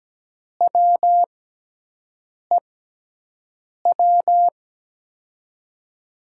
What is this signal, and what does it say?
Morse code, heard as a keyed tone, about 17 wpm, spaced as for 6 wpm, 700 Hz. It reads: WEW
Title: USA Call Signs - 5 Characters - 6x Character Spacing - 17wpm